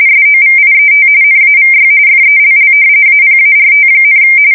RTTY, about 45.45 baud. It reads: SOVIET NAVY FELL APART,
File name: fsk170 / usnavy5